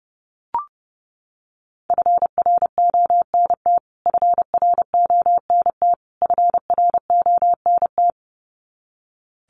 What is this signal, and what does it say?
Morse code, heard as a keyed tone, about 30 wpm, 700 Hz. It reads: FRONT FRONT FRONT
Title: Top 500 Words Repeated 30wpm